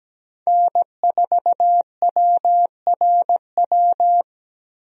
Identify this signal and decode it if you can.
Morse code, heard as a keyed tone, about 17 wpm, 700 Hz. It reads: N4WRW
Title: USA Call Signs - 5 Characters - Repeated 3x - 17wpm